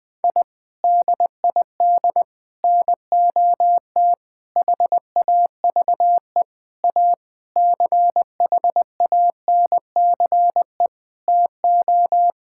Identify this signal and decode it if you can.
Morse code, heard as a keyed tone, about 21 wpm, 700 Hz. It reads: I DID NOT HAVE A CHANCE TO
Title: Sentences with 5-Word Phrases - Spoken Beginning - 20wpm